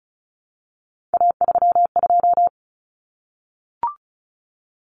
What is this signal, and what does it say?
Morse code, heard as a keyed tone, about 34 wpm, 700 Hz. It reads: A32
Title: Common Cut Numbers - 3-Digit 35wpm